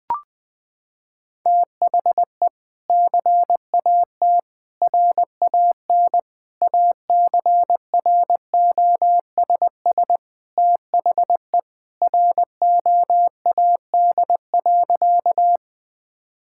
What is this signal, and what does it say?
Morse code, heard as a keyed tone, about 20 wpm, 700 Hz. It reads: THE CAT RAN ACROSS THE ROAD.
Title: Sentences with 2-Word Phrases - 20wpm